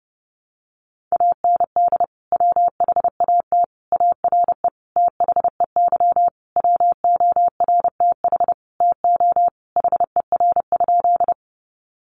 Morse code, heard as a keyed tone, about 30 wpm, 700 Hz. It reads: AND WHAT ARE THEY WORTH TO HER?
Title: Sentences from Top 800 Words 30wpm